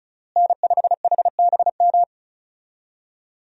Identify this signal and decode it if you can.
Morse code, heard as a keyed tone, about 36 wpm, 700 Hz. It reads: N5HBM
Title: USA Call Signs - 5 Characters - Repeated 3x - 35wpm